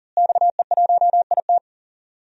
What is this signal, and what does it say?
Morse code, heard as a keyed tone, about 40 wpm, 690 Hz. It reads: XE1IT
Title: DXCC-100 Call Signs - 5 Characters - Repeated 2x - 40wpm